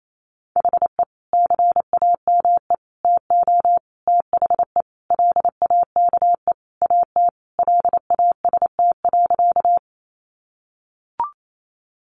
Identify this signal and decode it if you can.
Morse code, heard as a keyed tone, about 28 wpm, 700 Hz. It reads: HE CAME TO THE LAKE AT LAST.
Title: Sentences from Top 1100 Words 28wpm